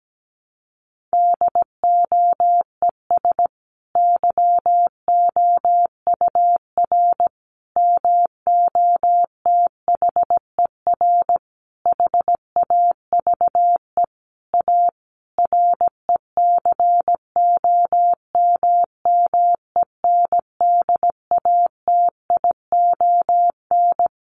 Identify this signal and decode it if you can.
Morse code, heard as a keyed tone, about 17 wpm, 700 Hz. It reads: DOES YOUR MOTHER HAVE A RECOMMENDATION
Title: Sentences from Top 1400 Words 17wpm